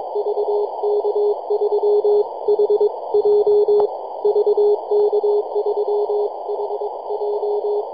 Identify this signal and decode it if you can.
Morse code, heard as a keyed tone, about 22 wpm, 420 Hz, with radio noise. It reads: VK3HJ VK3HJ
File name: yj0nk-18-075mhz-cw